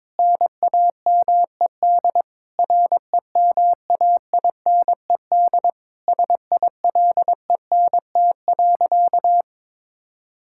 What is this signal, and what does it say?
Morse code, heard as a keyed tone, about 22 wpm, 700 Hz. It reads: NAMED REMAINED SILENT.
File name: Sentences with 7-Word Phrases - 22wpm